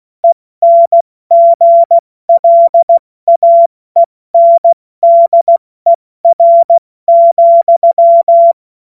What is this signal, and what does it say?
Morse code, heard as a keyed tone, about 16 wpm, spaced as for 14 wpm, 680 Hz. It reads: ENGLAENDER,
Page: cw-witze